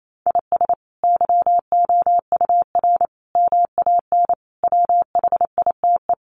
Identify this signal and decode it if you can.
Morse code, heard as a keyed tone, about 28 wpm, 700 Hz. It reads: IS YOUR MAN WHITE
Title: Sentences from Top 400 Words 28wpm